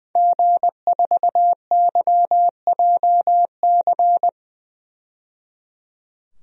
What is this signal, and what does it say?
Morse code, heard as a keyed tone, about 21 wpm, 700 Hz. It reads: G4YJC
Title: DXCC-100 Call Signs - 5 Characters - 20wpm